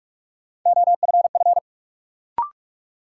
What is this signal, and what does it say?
Morse code, heard as a keyed tone, about 46 wpm, 700 Hz. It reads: OFF
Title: Sentences from Top 200 Words - New Word Drill - 45wpm